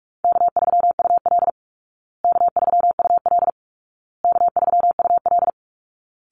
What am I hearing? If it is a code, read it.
Morse code, heard as a keyed tone, about 44 wpm, 700 Hz. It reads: K3UL K3UL K3UL
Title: Call Signs Repeated 2x - 45wpm